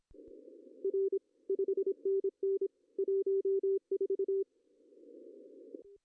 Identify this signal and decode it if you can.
Morse code, heard as a keyed tone, about 24 wpm, 390 Hz. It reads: R 5NN 14
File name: CW / cqww-cw-k5tr